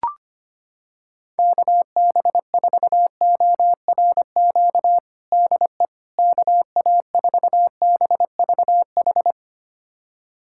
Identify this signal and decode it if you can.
Morse code, heard as a keyed tone, about 25 wpm, 700 Hz. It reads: KB4ORQ DE KA4BVH
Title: USA Callsign DE Callsign - 6 Characters - 25wpm